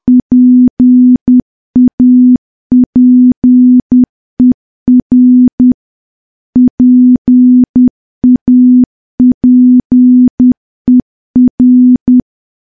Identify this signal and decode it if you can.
Morse code, heard as a keyed tone, about 10 wpm, 250 Hz. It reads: PAPER PAPER
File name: morse / morse-training-10WPM